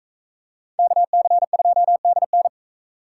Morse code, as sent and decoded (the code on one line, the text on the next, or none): -.- -.-. ..--- -.. -.
KC2DN